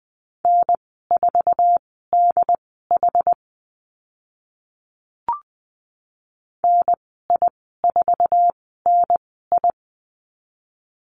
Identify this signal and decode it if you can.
Morse code, heard as a keyed tone, about 20 wpm, spaced as for 15 wpm, 700 Hz. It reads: N4DH NI4NI
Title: 076 Lesson 13g - Letter L - Call Signs 20wpm